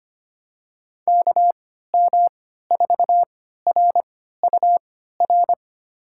Morse code, heard as a keyed tone, about 25 wpm, spaced as for 14 wpm, 700 Hz. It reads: KM4RUR
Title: USA Call Signs - 6 Characters - 3x Character Spacing - 25wpm